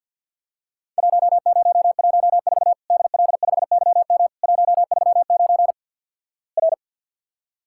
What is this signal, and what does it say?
Morse code, heard as a keyed tone, about 50 wpm, 700 Hz. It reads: 1014 DL5YM 139 R
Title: Worked All Europe DX Contest - QTC Exchange - 50wpm